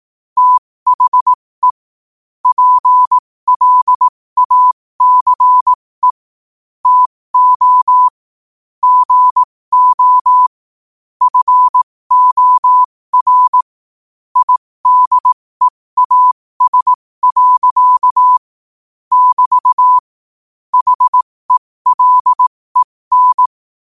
Morse code, heard as a keyed tone, about 18 wpm, spaced as for 15 wpm, 1000 Hz. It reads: THE PLACE TO GO FOR IDEAS. <BT> HELEN
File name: qotdfiles / 15wpm